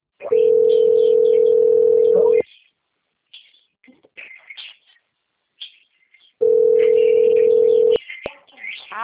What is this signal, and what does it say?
Ringback tone (North America), 440 + 480 Hz, with speech.
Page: Dracena 2 - SP